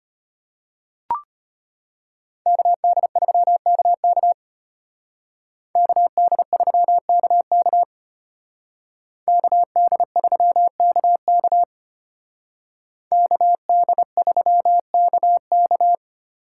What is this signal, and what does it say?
Morse code, heard as a keyed tone, about 33 wpm, 700 Hz. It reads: KD3KK KD3KK KD3KK KD3KK